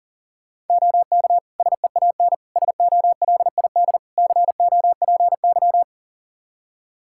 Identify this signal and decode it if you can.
Morse code, heard as a keyed tone, about 40 wpm, 700 Hz. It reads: OK SEAN SOLID COPY
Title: Practice QSO - Exchange 3 - More Info - 40wpm